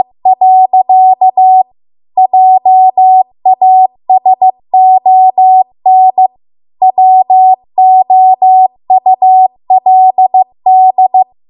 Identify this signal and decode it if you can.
Morse code, heard as a keyed tone, about 15 wpm, 750 Hz. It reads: . JASON WOULD